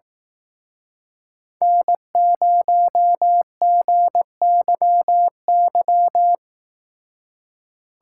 Morse code, heard as a keyed tone, about 18 wpm, 700 Hz. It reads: N0GYY
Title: Speed Racing - USA 1x3 Call Signs 15wpm